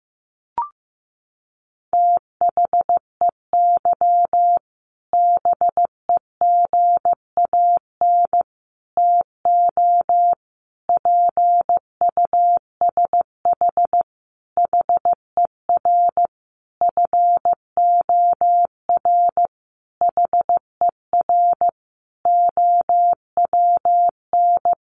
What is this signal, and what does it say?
Morse code, heard as a keyed tone, about 15 wpm, 700 Hz. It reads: THEY BEGAN TO PUSH HER FOR HER OWN
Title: Sentences from Top 1300 Words 15wpm